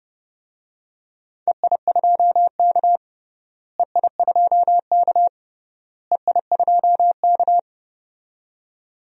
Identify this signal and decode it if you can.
Morse code, heard as a keyed tone, about 32 wpm, 700 Hz. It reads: EI2K EI2K EI2K
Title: DXCC-100 Call Signs - 4 Characters - Repeated 3x - 30wpm